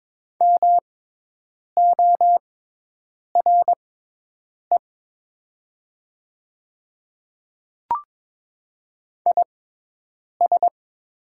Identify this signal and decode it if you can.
Morse code, heard as a keyed tone, about 22 wpm, spaced as for 8 wpm, 700 Hz. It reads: MORE IS
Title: 6x Character Spacing - Top 100 Words - 22wpm